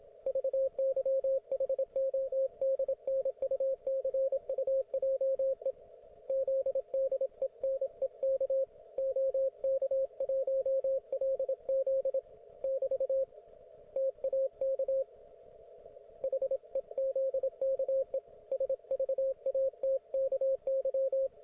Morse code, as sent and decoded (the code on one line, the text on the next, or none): ...- -.-- .... --- -.. -. ..- -.-. ..- .--- . / --.. -.. -. -.- / --- -.- .---- .-.. --.. / -...- / - .- -.- / .... --.. -.- . / ... ...- .- - -.- -.--
VYHODNUCUJE ZDENEK OK1LZ <BT> TAK HEZKE SVATKY